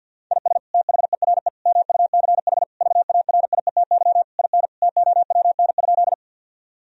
Morse code, heard as a keyed tone, about 50 wpm, 700 Hz. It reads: IS THERE MUCH VARIETY IN TOWN?
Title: Sentences from Top 1000 Words - Review Part 4 - 50wpm